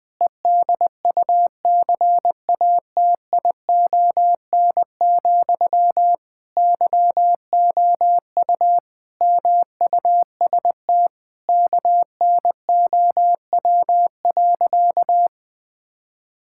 Morse code, heard as a keyed tone, about 20 wpm, 700 Hz. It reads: EDUCATION, YOU MUST KNOW.
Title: Sentences from Top 1000 Words - Review Part 6 - 20wpm